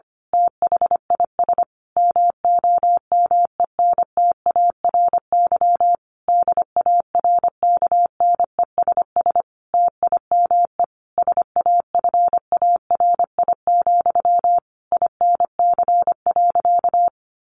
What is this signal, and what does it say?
Morse code, heard as a keyed tone, about 25 wpm, 700 Hz. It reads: THIS MOMENTARY DARKNESS TIME SAFARI, INC.
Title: A Sound Of Thunder 25wpm